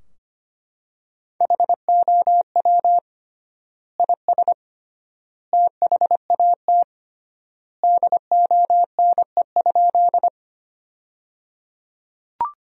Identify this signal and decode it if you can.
Morse code, heard as a keyed tone, about 25 wpm, 700 Hz. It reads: HOW IS THAT DONE?
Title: 3x Word-Spacing - Sentences from Top 500 Words - 25wpm